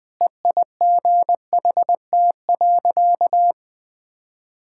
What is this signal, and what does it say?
Morse code, heard as a keyed tone, about 20 wpm, 700 Hz. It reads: EIGHT.
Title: Easy Sentences with Hints - 20wpm